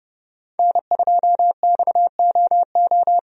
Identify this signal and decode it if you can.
Morse code, heard as a keyed tone, about 30 wpm, 700 Hz. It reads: N2XOO
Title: USA Call Signs - 5 Characters - 30wpm